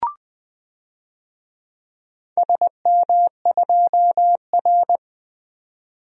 Morse code, heard as a keyed tone, about 20 wpm, 700 Hz. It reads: SM2R
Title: DXCC-100 Call Signs - 4 Characters - 20wpm